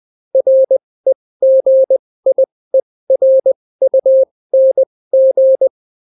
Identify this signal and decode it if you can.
Morse code, heard as a keyed tone, about 20 wpm, spaced as for 16 wpm, 530 Hz. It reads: REGIERUNG